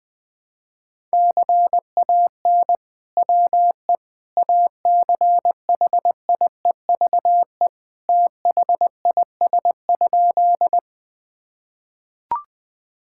Morse code, heard as a keyed tone, about 20 wpm, 700 Hz. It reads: CAN WE ACHIEVE THIS?
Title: Sentences from Top 1000 Words - Review Part 6 - 20wpm